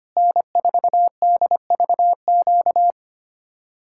Morse code, heard as a keyed tone, about 32 wpm, 700 Hz. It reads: N4DVQ